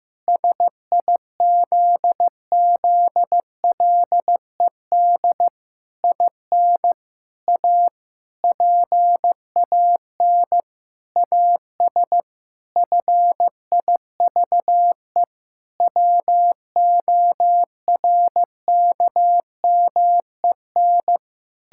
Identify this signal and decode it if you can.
Morse code, heard as a keyed tone, about 15 wpm, 700 Hz. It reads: SIZZLED IN A PAN AS FIVE WORKMEN
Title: Pangram Challenge v2 - 15wpm